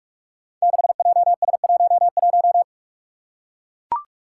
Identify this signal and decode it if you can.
Morse code, heard as a keyed tone, about 41 wpm, 700 Hz. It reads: 6JS11
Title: Five Challenging Letter-Number Combinations 45wpm